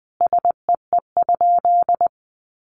Morse code, heard as a keyed tone, about 20 wpm, 700 Hz. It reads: SEE?
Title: Sentences from Top 100 Words 20wpm